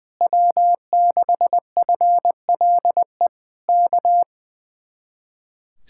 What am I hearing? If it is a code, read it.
Morse code, heard as a keyed tone, about 20 wpm, 700 Hz. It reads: W6FLE K
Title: Speed-Racing Common QSO Phrases Encore - 20wpm